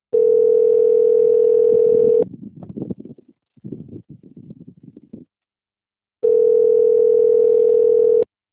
Ringback tone (North America), 440 + 480 Hz.
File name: out- - - - - .1900